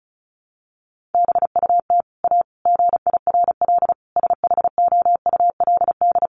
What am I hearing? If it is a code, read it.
Morse code, heard as a keyed tone, about 34 wpm, 700 Hz. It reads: BUT A GIRL SHOULD
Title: Sentences from Top 1200 Words 35wpm